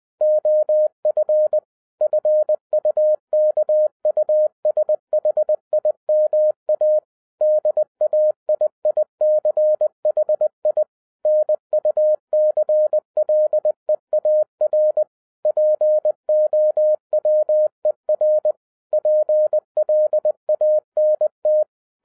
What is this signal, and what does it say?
Morse code, heard as a keyed tone, about 20 wpm, 610 Hz. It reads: OF FUKUSHIMA DAIICHI NUCLEAR POWER PLANT
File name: 2011-10-mp3 / n03-TARO JR0QWW 20wpm